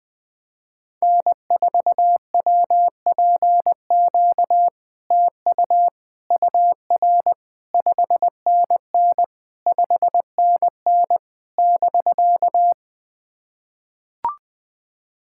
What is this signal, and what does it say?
Morse code, heard as a keyed tone, about 20 wpm, 700 Hz. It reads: N4WPQ TU UR 5NN 5NN <BK>
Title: SOTA QSO Practice 20wpm